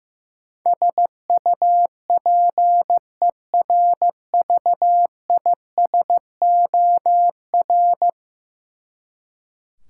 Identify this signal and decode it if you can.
Morse code, heard as a keyed tone, about 15 wpm, 700 Hz. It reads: SUPERVISOR
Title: Practice QSO - Exchange 3 - More Info - 15wpm